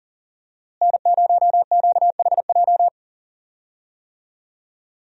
Morse code, heard as a keyed tone, about 40 wpm, 700 Hz. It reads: N0QHJ
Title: USA Call Signs - 5 Characters - Repeated 2x - 40wpm